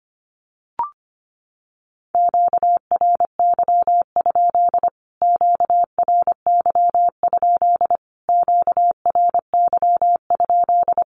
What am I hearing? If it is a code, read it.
Morse code, heard as a keyed tone, about 25 wpm, 700 Hz. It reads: QRY? QRY? QRY?
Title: Q-Codes Repeated 3x 25wpm